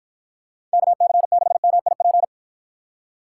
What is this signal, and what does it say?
Morse code, heard as a keyed tone, about 53 wpm, 700 Hz. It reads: KC6MIP